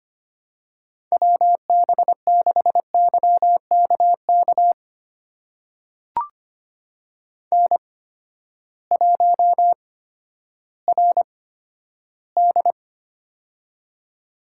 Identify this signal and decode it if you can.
Morse code, heard as a keyed tone, about 25 wpm, spaced as for 12 wpm, 700 Hz. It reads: WB6YKK N1RD
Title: USA Call Signs - 8x Character Spacing - 25wpm